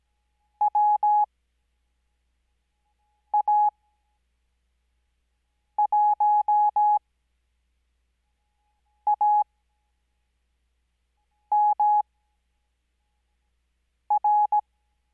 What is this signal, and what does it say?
Morse code, heard as a keyed tone, about 17 wpm, spaced as for 4 wpm, 840 Hz. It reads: WA1AMR
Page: Farnsworth Spacing: 15WPM Character speed Spaced at 5WPM.